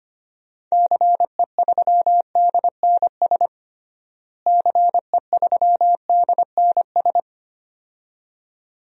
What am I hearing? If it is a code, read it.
Morse code, heard as a keyed tone, about 25 wpm, 700 Hz. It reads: CE3DNS CE3DNS
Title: DXCC-100 Call Signs - 6 Characters - Repeated 2x - 25wpm